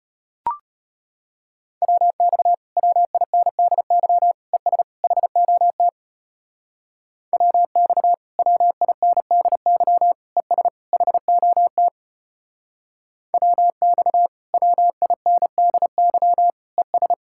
Morse code, heard as a keyed tone, about 33 wpm, 700 Hz. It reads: WX WINDY ES HOT WX WINDY ES HOT WX WINDY ES